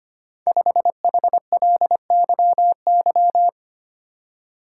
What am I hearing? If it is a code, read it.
Morse code, heard as a keyed tone, about 25 wpm, 700 Hz. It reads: 5HLYY
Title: Code Groups - 5 Letter:Numbers - 25wpm